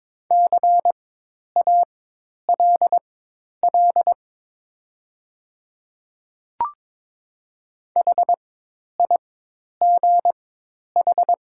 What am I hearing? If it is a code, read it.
Morse code, heard as a keyed tone, about 22 wpm, spaced as for 10 wpm, 700 Hz. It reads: CALL HIGH
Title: 4x Character Spacing - Top 500 Words - 22wpm